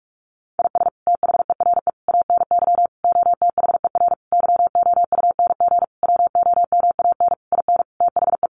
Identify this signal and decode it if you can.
Morse code, heard as a keyed tone, about 46 wpm, 700 Hz. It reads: IS THERE ANY OTHER YOUNG WOMAN IN THE
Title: Sentences from Top 1100 Words 45wpm